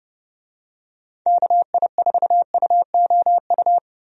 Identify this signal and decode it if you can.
Morse code, heard as a keyed tone, about 32 wpm, 700 Hz. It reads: KI4UOU
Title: USA Call Signs - 6 Characters - Repeated 3x - 30wpm